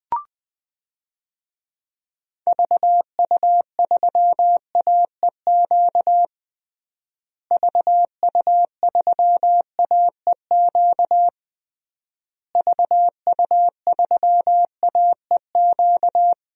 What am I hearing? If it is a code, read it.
Morse code, heard as a keyed tone, about 20 wpm, 700 Hz. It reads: VU3AEQ VU3AEQ VU3AEQ
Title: DXCC-100 Call Signs - 6 Characters - Repeated 3x - 20wpm